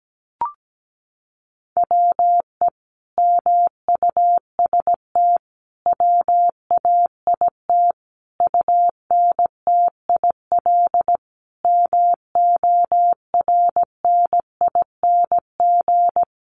Morse code, heard as a keyed tone, about 17 wpm, 700 Hz. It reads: WE MUST WAIT UNTIL MORNING
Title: Sentences from Top 1300 Words 17wpm